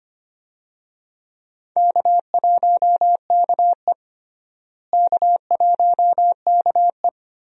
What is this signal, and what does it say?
Morse code, heard as a keyed tone, about 25 wpm, 700 Hz. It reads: K1KE K1KE
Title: USA Call Signs - 4 Characters - Repeated 3x - 25wpm